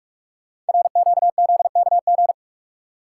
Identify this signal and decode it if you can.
Morse code, heard as a keyed tone, about 44 wpm, 700 Hz. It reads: RQZKG
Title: Code Groups - 5 Letter:Numbers - 45wpm